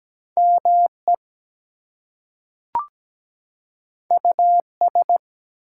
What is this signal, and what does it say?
Morse code, heard as a keyed tone, about 17 wpm, 700 Hz. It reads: ME US
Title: Two Letter Words - 17wpm